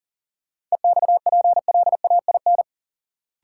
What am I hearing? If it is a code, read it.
Morse code, heard as a keyed tone, about 40 wpm, 700 Hz. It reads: EXPLAIN
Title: Sentences from Top 1100 Words - New Word Drill - 40wpm